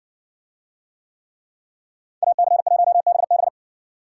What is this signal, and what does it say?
Morse code, heard as a keyed tone, about 56 wpm, 700 Hz. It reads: AC1BB